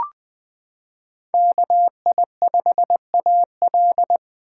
Morse code, heard as a keyed tone, about 20 wpm, 700 Hz. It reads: KI5AL
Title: K1USN SST QSO Practice - 20wpm